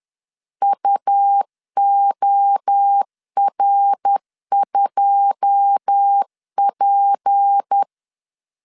Morse code, heard as a keyed tone, about 10 wpm, 800 Hz. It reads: UOR2P
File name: CW / DET / 1 9 1500C S A 10wpm 4511